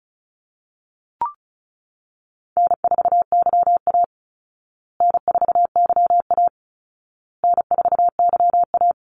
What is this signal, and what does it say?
Morse code, heard as a keyed tone, about 34 wpm, 700 Hz. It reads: N4YA N4YA N4YA
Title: Call Signs Repeated 2x - 35wpm